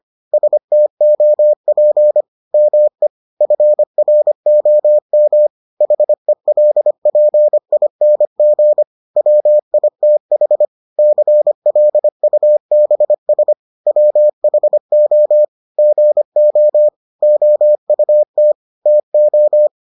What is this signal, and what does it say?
Morse code, heard as a keyed tone, about 25 wpm, 600 Hz. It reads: STOP ME FROM HELPING WITH CLUBS WHO GO OUT TO